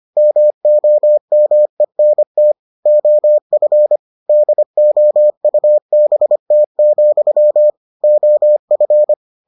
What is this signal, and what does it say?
Morse code, heard as a keyed tone, about 25 wpm, 600 Hz. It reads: MOMENT OF DOUBT, OF